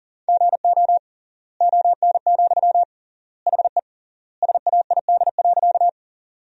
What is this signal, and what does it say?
Morse code, heard as a keyed tone, about 40 wpm, 700 Hz. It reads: GO ON, HE SAID.
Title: 3x Word-Spacing - Sentences from Top 100 Words - 40wpm